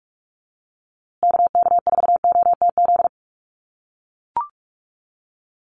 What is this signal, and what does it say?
Morse code, heard as a keyed tone, about 44 wpm, 700 Hz. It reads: KK4OTZ
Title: USA Call Signs Repeated 3x 45wpm